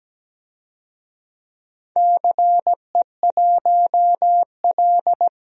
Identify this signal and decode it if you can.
Morse code, heard as a keyed tone, about 17 wpm, 700 Hz. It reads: CE1L
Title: DXCC-100 Call Signs - 4 Characters - Repeated 3x - 17wpm